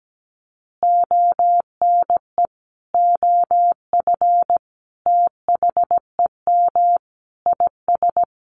Morse code, heard as a keyed tone, about 17 wpm, 700 Hz. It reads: ONE OF THEM IS
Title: Additional Easy Phrases 17wpm